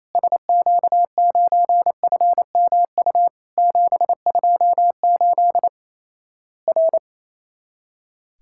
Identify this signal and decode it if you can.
Morse code, heard as a keyed tone, about 28 wpm, 700 Hz. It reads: SQ9FMU 728 R
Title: Worked All Europe DX Contest - QTC Exchange - 28wpm